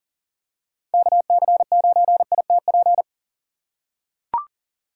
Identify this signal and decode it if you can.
Morse code, heard as a keyed tone, about 40 wpm, 700 Hz. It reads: KC9ITP
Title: USA Call Signs - 3x Character Spacing - 40wpm